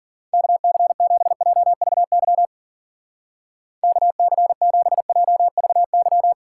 Morse code, heard as a keyed tone, about 50 wpm, 700 Hz. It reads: KC7JVY KC7JVY